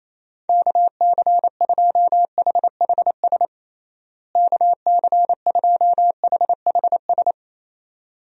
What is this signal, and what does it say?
Morse code, heard as a keyed tone, about 28 wpm, 700 Hz. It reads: KC2HHS KC2HHS
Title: USA Call Signs - 6 Characters - Repeated 2x - 28wpm